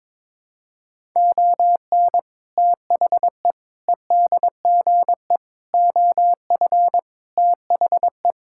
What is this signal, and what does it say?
Morse code, heard as a keyed tone, about 22 wpm, 700 Hz. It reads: ON THE EDGE OF THE
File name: practice / 22wpm / Sentences with 4-Word Phrases - 22wpm